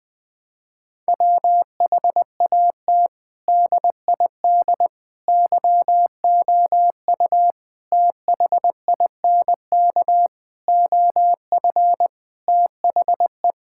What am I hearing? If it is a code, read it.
Morse code, heard as a keyed tone, about 20 wpm, 700 Hz. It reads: WHAT DID YOU THINK OF THE
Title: Sentences from Top 1000 Words - Review Part 5 - 20wpm